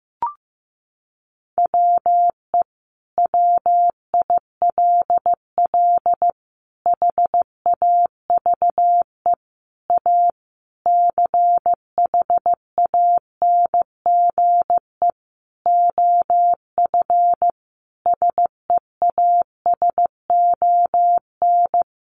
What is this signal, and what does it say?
Morse code, heard as a keyed tone, about 15 wpm, 700 Hz. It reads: WE WILL HAVE A CHANGE OF SEASON
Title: Sentences from Top 1400 Words 15wpm